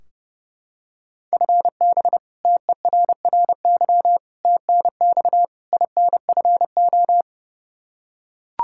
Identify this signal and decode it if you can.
Morse code, heard as a keyed tone, about 29 wpm, 700 Hz. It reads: FB TERRY TNX INFO
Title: Practice QSO - Exchange 3 - More Info - 30wpm